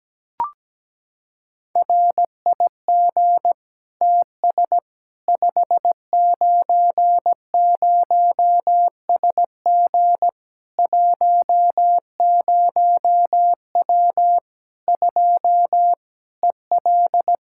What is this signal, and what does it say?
Morse code, heard as a keyed tone, about 17 wpm, 700 Hz. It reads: RIG TS 590SG 10W 2 EL
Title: Practice QSO - Exchange 2 - Info - 17wpm